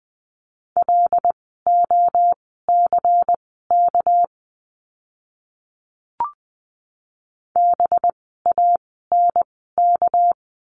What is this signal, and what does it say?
Morse code, heard as a keyed tone, about 20 wpm, spaced as for 14 wpm, 700 Hz. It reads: LOCK BANK
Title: 199 Lesson 32f - Letter k - Words 20wpm